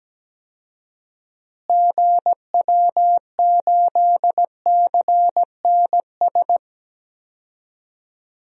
Morse code, heard as a keyed tone, about 17 wpm, 700 Hz. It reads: GW8CNS
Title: DXCC-100 Call Signs - 6 Characters - Repeated 2x - 17wpm